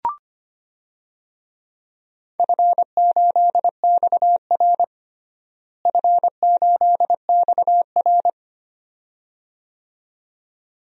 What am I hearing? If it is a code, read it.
Morse code, heard as a keyed tone, about 25 wpm, 700 Hz. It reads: F8XR F8XR
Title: DXCC-100 Call Signs - 4 Characters - Repeated 2x - 25wpm